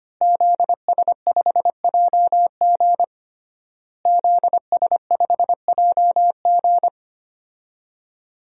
Morse code, heard as a keyed tone, about 25 wpm, 700 Hz. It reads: ZS5JG ZS5JG
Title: DXCC-100 Call Signs - 5 Characters - Repeated 3x - 25wpm